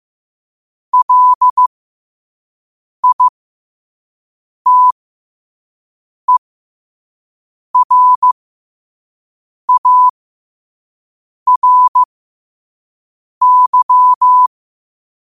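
Morse code, heard as a keyed tone, about 15 wpm, spaced as for 5 wpm, 1000 Hz. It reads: LITERARY
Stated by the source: Morse code practice at 5 wpm.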